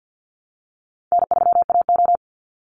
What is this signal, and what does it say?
Morse code, heard as a keyed tone, about 50 wpm, 700 Hz. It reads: N3AO
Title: Easy Call Signs Repeated 2x - 50wpm